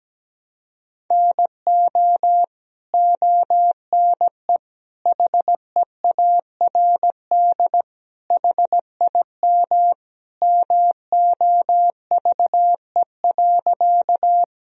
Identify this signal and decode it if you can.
Morse code, heard as a keyed tone, about 17 wpm, 700 Hz. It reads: NO ONE HEARD HIM MOVE.
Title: Sentences from Top 1000 Words - Review Part 1 - 17wpm